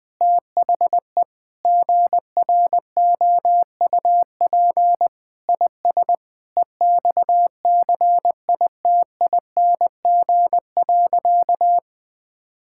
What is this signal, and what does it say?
Morse code, heard as a keyed tone, about 20 wpm, 700 Hz. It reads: THE GROUP IS EXCITING.
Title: Sentences with 4-Word Phrases - Spoken Beginning - 20wpm